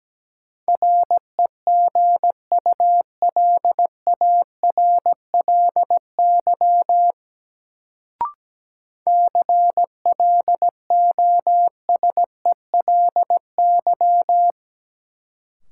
Morse code, heard as a keyed tone, about 17 wpm, 700 Hz. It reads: REGULARLY CLOSELY
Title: Sentences from Top 1400 Words - New Word Drill - 17wpm